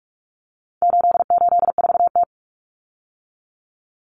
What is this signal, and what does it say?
Morse code, heard as a keyed tone, about 44 wpm, 700 Hz. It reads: 884T
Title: Common Cut Numbers - 4-Digit 45wpm